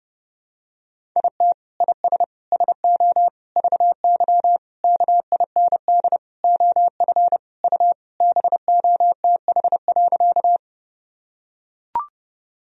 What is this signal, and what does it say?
Morse code, heard as a keyed tone, about 30 wpm, 700 Hz. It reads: IT IS SO VY KIND OF U BOTH.
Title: Sentences from Top 300 Words and Abbreviations 30wpm